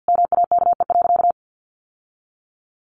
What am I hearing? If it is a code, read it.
Morse code, heard as a keyed tone, about 50 wpm, 700 Hz. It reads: MAKE.
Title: Sentences from Top 1100 Words 50wpm